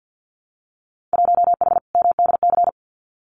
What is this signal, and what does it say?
Morse code, heard as a keyed tone, about 50 wpm, 700 Hz. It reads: 1H MDC
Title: Winter Field Day Exchanges 50wpm